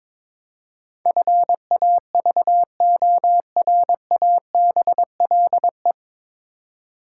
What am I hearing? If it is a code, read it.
Morse code, heard as a keyed tone, about 22 wpm, 700 Hz. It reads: FAVORABLE
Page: Morse Code Ninja practice 22wpm